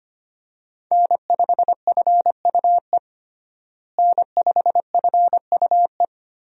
Morse code, heard as a keyed tone, about 25 wpm, 700 Hz. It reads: N5FUE N5FUE
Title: USA Call Signs - 5 Characters - Repeated 2x - 25wpm